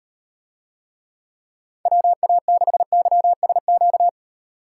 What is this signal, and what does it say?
Morse code, heard as a keyed tone, about 38 wpm, 700 Hz. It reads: WA6YSQ